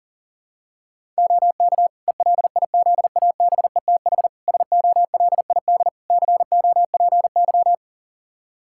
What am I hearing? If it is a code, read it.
Morse code, heard as a keyed tone, about 40 wpm, 700 Hz. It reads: OK ELIZABETH SOLID COPY
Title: Practice QSO - Exchange 3 - More Info - 40wpm